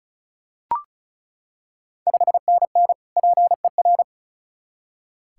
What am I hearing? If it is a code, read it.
Morse code, heard as a keyed tone, about 36 wpm, 690 Hz. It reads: 5NN PER
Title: Ontario QSO Party Exchange 35wpm